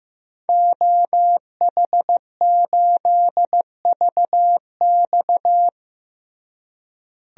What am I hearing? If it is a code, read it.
Morse code, heard as a keyed tone, about 15 wpm, 700 Hz. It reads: OH8VX
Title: DXCC-100 Call Signs - 5 Characters - Repeated 2x - 15wpm